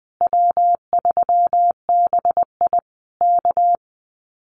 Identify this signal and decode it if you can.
Morse code, heard as a keyed tone, about 20 wpm, 700 Hz. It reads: W3BI K
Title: Common QSO Phrases Encore 20wpm